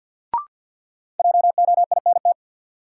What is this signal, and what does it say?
Morse code, heard as a keyed tone, about 53 wpm, 700 Hz. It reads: JOINT
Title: Sentences from Top 1300 Words - New Word Drill - 50wpm